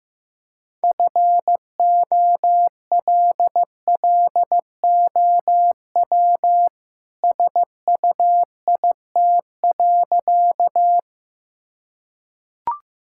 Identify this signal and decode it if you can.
Morse code, heard as a keyed tone, about 15 wpm, 700 Hz. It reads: FOLLOW SUIT.
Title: Sentences from Top 1600 Words 15wpm